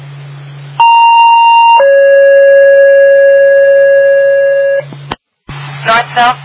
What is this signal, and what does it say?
Two-tone page, heard as a one-second tone, then a three-second tone, about 930 then 550 Hz, with speech. It is heard from 0.8 to 4.8 s.